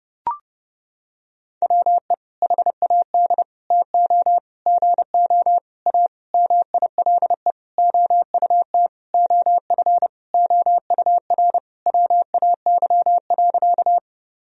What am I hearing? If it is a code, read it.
Morse code, heard as a keyed tone, about 30 wpm, 700 Hz. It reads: WE HAD TO GO A MILE OUT OF OUR WAY.
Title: Sentences from Top 1000 Words - Review Part 1 - 30wpm